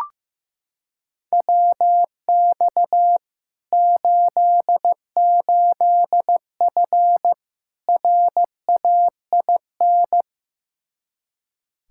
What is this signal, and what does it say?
Morse code, heard as a keyed tone, about 15 wpm, 700 Hz. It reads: WX 88F RAIN
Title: Practice QSO - Exchange 2 - Info - 15wpm